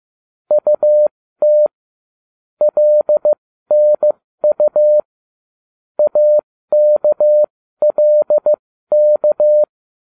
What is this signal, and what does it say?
Morse code, heard as a keyed tone, about 15 wpm, spaced as for 13 wpm, 600 Hz. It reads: UT LNU AKLK